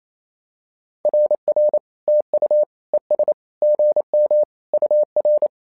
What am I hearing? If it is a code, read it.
Morse code, heard as a keyed tone, about 29 wpm, 600 Hz. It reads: RR TU ES GM UR
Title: SOTA QSO Practice 28wpm